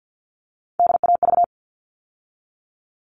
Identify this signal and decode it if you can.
Morse code, heard as a keyed tone, about 50 wpm, 700 Hz. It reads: DAV
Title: All Cut Numbers - 3-Digit 50wpm